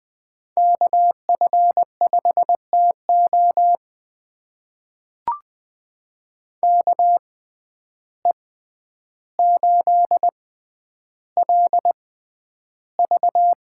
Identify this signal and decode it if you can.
Morse code, heard as a keyed tone, about 20 wpm, spaced as for 10 wpm, 700 Hz. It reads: KF5TO KE8LV
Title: USA Call Signs - 5 Characters - 6x Character Spacing - 20wpm